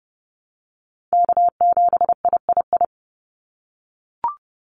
Morse code, heard as a keyed tone, about 30 wpm, 700 Hz. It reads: K7III
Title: NA SOTA Chaser Callsigns 30wpm